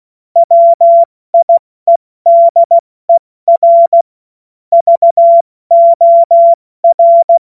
Morse code, heard as a keyed tone, about 16 wpm, spaced as for 14 wpm, 680 Hz. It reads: WIEDER VOR
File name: Morsekurs / Witze / Witz 05 März 2024 70 F80